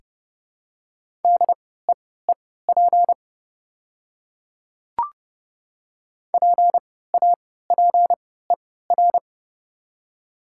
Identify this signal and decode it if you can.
Morse code, heard as a keyed tone, about 30 wpm, spaced as for 17 wpm, 690 Hz. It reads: DEEP PAPER
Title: 3x Character Spacing - Top 500 Words - 30wpm